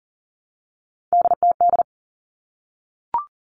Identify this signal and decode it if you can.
Morse code, heard as a keyed tone, about 40 wpm, 700 Hz. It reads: DTD
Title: All Cut Numbers - 3-Digit 40wpm